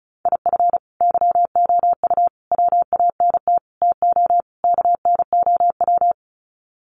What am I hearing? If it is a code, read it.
Morse code, heard as a keyed tone, about 34 wpm, 700 Hz. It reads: IF YOU WANT TO KNOW
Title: Sets of 5 Words Encore2 35wpm